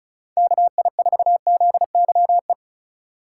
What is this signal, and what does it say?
Morse code, heard as a keyed tone, about 34 wpm, 700 Hz. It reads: KI4ZYE
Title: USA Call Signs - 6 Characters - 4x Character Spacing - 35wpm